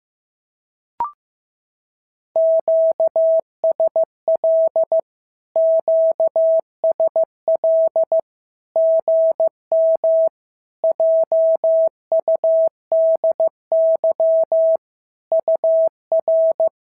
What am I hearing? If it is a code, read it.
Morse code, heard as a keyed tone, about 15 wpm, 650 Hz. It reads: QSL QSL GM JUDY UR
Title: SOTA QSO Practice 15wpm